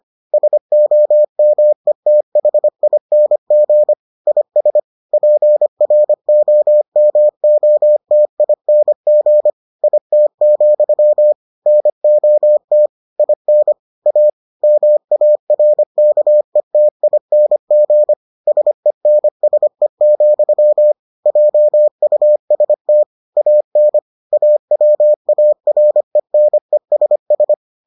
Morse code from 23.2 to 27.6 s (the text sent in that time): AN AWARENESS